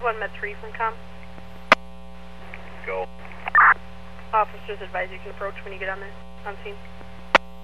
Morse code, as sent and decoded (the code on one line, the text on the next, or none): none